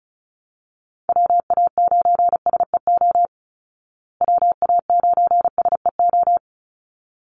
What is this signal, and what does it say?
Morse code, heard as a keyed tone, about 34 wpm, 700 Hz. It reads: WA9SEO WA9SEO
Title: Call Signs Repeated 3x - 35wpm